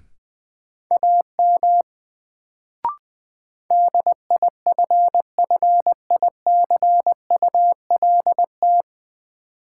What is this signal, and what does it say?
Morse code, heard as a keyed tone, about 21 wpm, 700 Hz. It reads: AM DIFFICULT
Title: Sentences from Top 600 Words - New Word Drill - 20wpm